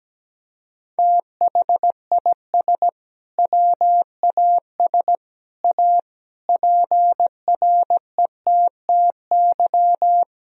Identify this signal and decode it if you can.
Morse code, heard as a keyed tone, about 17 wpm, 700 Hz. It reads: THIS WAS A PRETTY
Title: Sentences from Top 1000 Words - Review Part 2 - 17wpm